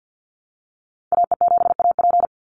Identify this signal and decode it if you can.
Morse code, heard as a keyed tone, about 50 wpm, 700 Hz. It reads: AE7AP
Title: NA SOTA Activator Callsigns 50wpm